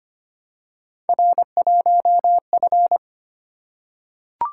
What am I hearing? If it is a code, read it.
Morse code, heard as a keyed tone, about 25 wpm, 700 Hz. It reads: R1F
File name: Three Letter-Number Combinations - 25wpm